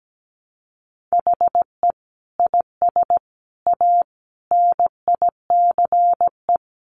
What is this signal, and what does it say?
Morse code, heard as a keyed tone, about 17 wpm, 700 Hz. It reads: HE IS A NICE
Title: Sentences from Top 1100 Words 17wpm